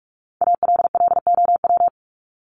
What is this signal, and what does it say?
Morse code, heard as a keyed tone, about 44 wpm, 700 Hz. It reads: ALLOW
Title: Top 501-750 Words 45wpm